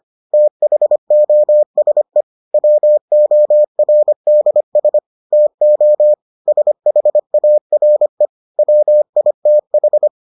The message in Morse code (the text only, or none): THOSE WORDS TO SHARE WITH